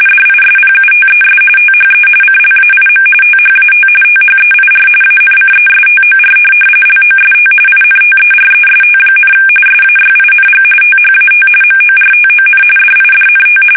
RTTY, about 45.45 baud. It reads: DISPUTES THE CLAIM THAT THEIR ANCESTOR LOST INTEREST, SAYING THAT MORTON DIDNT
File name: fsk850 / teletype-corp